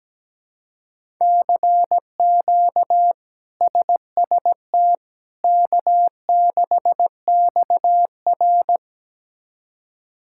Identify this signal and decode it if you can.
Morse code, heard as a keyed tone, about 17 wpm, 700 Hz. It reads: CQ SST K6XR
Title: K1USN SST QSO Practice - 17wpm